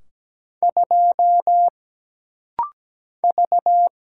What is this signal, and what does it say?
Morse code, heard as a keyed tone, about 17 wpm, 700 Hz. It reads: 2 V
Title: Single Letter-Number - Rapid-Fire 17wpm